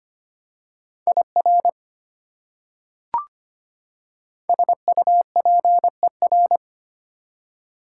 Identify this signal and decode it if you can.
Morse code, heard as a keyed tone, about 25 wpm, 700 Hz. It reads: IR SUPER
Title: Prefixes and Suffixes 25wpm